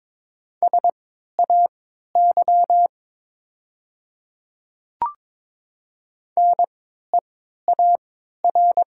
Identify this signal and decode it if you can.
Morse code, heard as a keyed tone, about 22 wpm, spaced as for 13 wpm, 700 Hz. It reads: SAY NEAR
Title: 3x Character Spacing - Top 300 Words - 22wpm